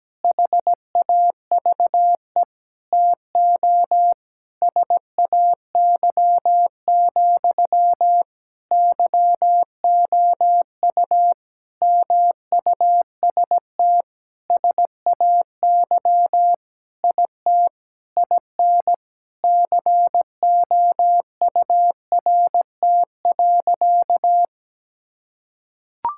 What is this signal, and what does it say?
Morse code, heard as a keyed tone, about 17 wpm, 700 Hz. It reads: HAVE TO SAY, YOU MUST SAY IT IN COURT.
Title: Sentences from Top 1600 Words 17wpm